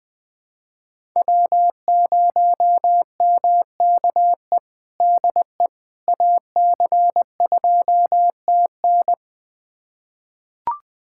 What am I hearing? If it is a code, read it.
Morse code, heard as a keyed tone, about 21 wpm, 700 Hz. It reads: W0MKE DE AC2TN
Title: USA Callsign DE Callsign - 5 Characters - 20wpm